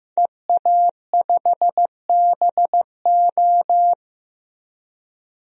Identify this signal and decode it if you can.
Morse code, heard as a keyed tone, about 15 wpm, 700 Hz. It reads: EA5BO
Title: DXCC-100 Call Signs - 5 Characters - 15wpm